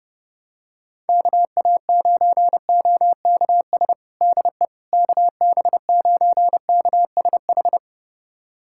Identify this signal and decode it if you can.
Morse code, heard as a keyed tone, about 30 wpm, 700 Hz. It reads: KA9OKS DE KB9KSH
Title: USA Callsign DE Callsign - 6 Characters - 30wpm